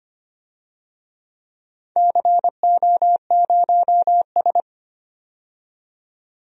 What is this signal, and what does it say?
Morse code, heard as a keyed tone, about 25 wpm, 700 Hz. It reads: CO0S
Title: DXCC-100 Call Signs - 4 Characters - 25wpm